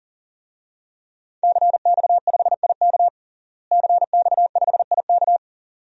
Morse code, heard as a keyed tone, about 40 wpm, 700 Hz. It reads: CX5IK CX5IK
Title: DXCC-100 Call Signs - 5 Characters - Repeated 2x - 40wpm